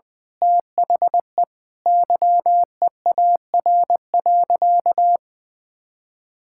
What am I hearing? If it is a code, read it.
Morse code, heard as a keyed tone, about 20 wpm, 700 Hz. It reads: THE YEAR.
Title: Sentences with 8-Word Phrases - 20wpm